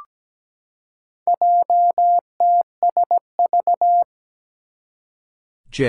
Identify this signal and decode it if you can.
Morse code, heard as a keyed tone, about 17 wpm, 700 Hz. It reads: JTSV
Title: Four Challenging Letter-Number Combinations 17wpm